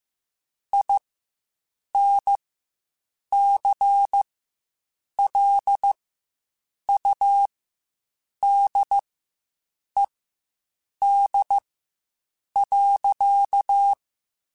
Morse code, heard as a keyed tone, about 15 wpm, spaced as for 7 wpm, 800 Hz. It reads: INCLUDED.
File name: Quotes Spaced / Quotes Spaced-mp3s / Airtight sp @10